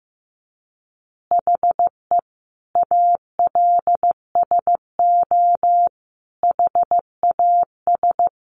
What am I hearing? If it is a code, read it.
Morse code, heard as a keyed tone, about 15 wpm, 700 Hz. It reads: HE ALSO HAS
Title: Additional Easy Phrases 15wpm